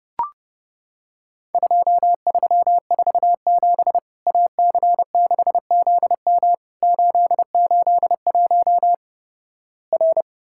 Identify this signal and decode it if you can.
Morse code, heard as a keyed tone, about 32 wpm, 700 Hz. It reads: 2347 AC6ZM 881 R
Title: Worked All Europe DX Contest - QTC Exchange - 30wpm